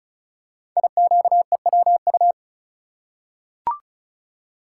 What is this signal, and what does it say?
Morse code, heard as a keyed tone, about 36 wpm, 700 Hz. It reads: IQEWU
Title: Code Groups - 5 Letters - 35wpm